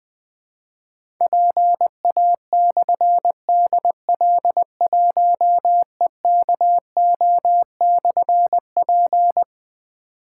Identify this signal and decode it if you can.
Morse code, heard as a keyed tone, about 20 wpm, 700 Hz. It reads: PA/DL1EKO/P
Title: Difficult Call Signs 20wpm